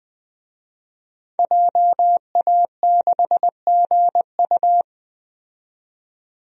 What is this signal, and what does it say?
Morse code, heard as a keyed tone, about 20 wpm, 700 Hz. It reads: JA6GU
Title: DXCC-100 Call Signs - 5 Characters - 20wpm